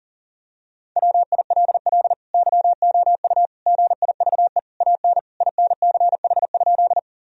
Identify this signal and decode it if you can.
Morse code, heard as a keyed tone, about 40 wpm, 700 Hz. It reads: WILL YOU GIVE AN INCH?
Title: Sentences from Top 1000 Words - Review Part 3 - 40wpm